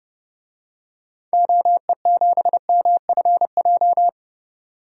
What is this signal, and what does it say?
Morse code, heard as a keyed tone, about 30 wpm, 700 Hz. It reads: OE7MFJ
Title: DXCC-100 Call Signs - 6 Characters - 30wpm